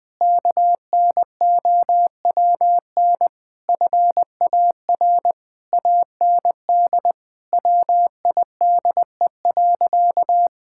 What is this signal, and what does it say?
Morse code, heard as a keyed tone, about 20 wpm, 700 Hz. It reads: KNOWN FAR AND WIDE.
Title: Sentences from Top 1000 Words - Review Part 6 - 20wpm